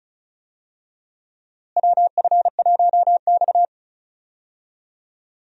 Morse code, heard as a keyed tone, about 34 wpm, 700 Hz. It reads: WF1X
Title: USA Call Signs - 4 Characters - Repeated 2x - 35wpm